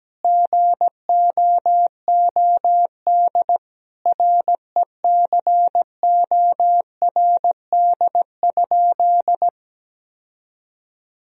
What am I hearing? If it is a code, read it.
Morse code, heard as a keyed tone, about 17 wpm, 700 Hz. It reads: GOOD RECORD?
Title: Sentences from Top 1000 Words - Review Part 3 - 17wpm